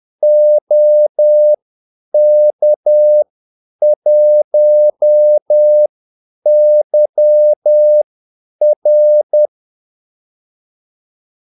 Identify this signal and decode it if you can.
Morse code, heard as a keyed tone, about 10 wpm, spaced as for 8 wpm, 600 Hz. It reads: OK1YR
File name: Callsigns-10-8wpm